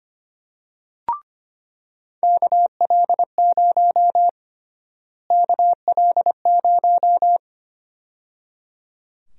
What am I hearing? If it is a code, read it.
Morse code, heard as a keyed tone, about 25 wpm, 700 Hz. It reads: KL0 KL0
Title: USA Call Sign Prefixes - Repeated 2x - 25wpm